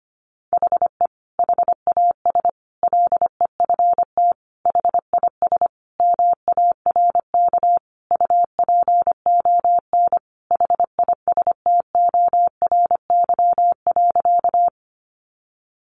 Morse code, heard as a keyed tone, about 25 wpm, 700 Hz. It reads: HE HAS LEFT HIS MARK UPON HISTORY.
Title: Sentences from Top 1100 Words 25wpm